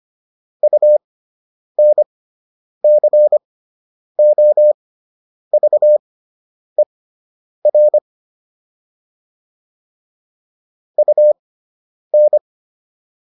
Morse code, heard as a keyed tone, about 25 wpm, spaced as for 9 wpm, 600 Hz. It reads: UNCOVER UN